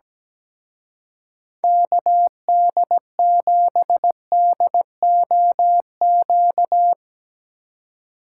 Morse code, heard as a keyed tone, about 18 wpm, 700 Hz. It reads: KD7DOQ
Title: Speed Racing - USA 2x3 Call Signs 17wpm